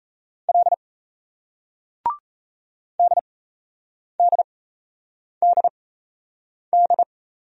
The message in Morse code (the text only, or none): L DDDD